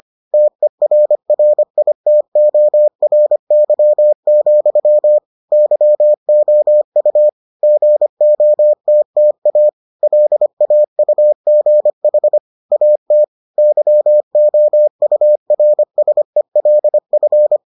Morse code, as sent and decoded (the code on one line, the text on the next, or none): - . .-. .-. .. - --- .-. -.-- --..-- / -.-- --- ..- / --. --- - - .- / .-.. .- ..- --. .... / .- - / -.-- --- ..- .-. ... .-.. ..-.
TERRITORY, YOU GOTTA LAUGH AT YOURSELF